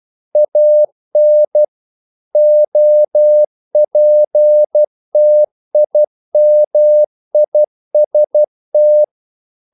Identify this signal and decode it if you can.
Morse code, heard as a keyed tone, about 12 wpm, 600 Hz. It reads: AN OPTIMIST
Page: English quotations in CW (telegraphy - morse)